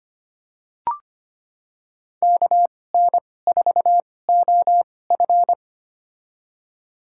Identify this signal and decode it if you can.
Morse code, heard as a keyed tone, about 25 wpm, spaced as for 18 wpm, 700 Hz. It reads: KN4OF
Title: USA Call Signs - 5 Characters - 2x Character Spacing - 25wpm